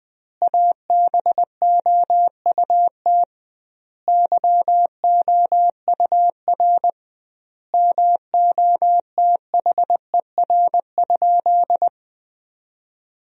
Morse code, heard as a keyed tone, about 20 wpm, 700 Hz. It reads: ABOUT YOUR MOTHER?
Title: 2x Word-Spacing - Sentences from Top 200 Words - 20wpm